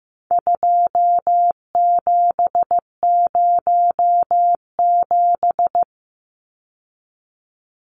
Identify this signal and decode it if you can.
Morse code, heard as a keyed tone, about 15 wpm, 700 Hz. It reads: 2707
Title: Numbers - 4-Digit 15wpm